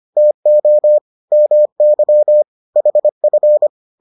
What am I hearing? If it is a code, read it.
Morse code, heard as a keyed tone, about 25 wpm, 600 Hz. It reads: TO MY HF